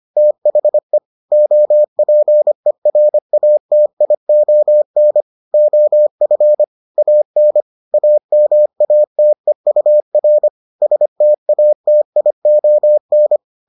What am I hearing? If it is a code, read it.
Morse code, heard as a keyed tone, about 25 wpm, 600 Hz. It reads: THE OPERATION OF AN AMATEUR STATION